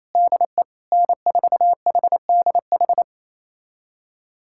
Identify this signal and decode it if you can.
Morse code, heard as a keyed tone, about 28 wpm, 700 Hz. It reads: DE N4HDH
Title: USA Callsign DE Callsign - 5 Characters - 28wpm